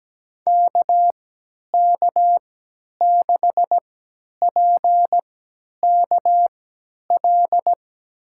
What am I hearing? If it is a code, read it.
Morse code, heard as a keyed tone, about 17 wpm, spaced as for 10 wpm, 700 Hz. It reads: KK6PKL
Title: USA Call Signs - 6 Characters - 3x Character Spacing - 17wpm